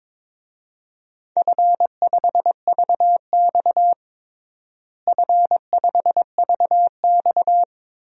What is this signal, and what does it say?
Morse code, heard as a keyed tone, about 22 wpm, 700 Hz. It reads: F5VX F5VX
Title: DXCC-100 Call Signs - 4 Characters - Repeated 3x - 22wpm